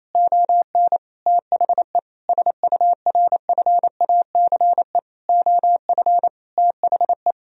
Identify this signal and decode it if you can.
Morse code, heard as a keyed tone, about 28 wpm, 700 Hz. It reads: ON THE SURFACE OF THE
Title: Sentences from Top 1000 Words - Review Part 3 - 28wpm